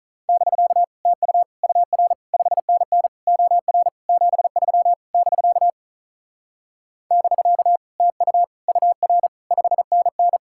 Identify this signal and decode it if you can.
Morse code, heard as a keyed tone, about 44 wpm, 700 Hz. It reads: <BK> TU UR 5NN OR 73 <BK> <BK> TU UR 5NN
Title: Speed-Racing Common QSO Phrases - 35wpm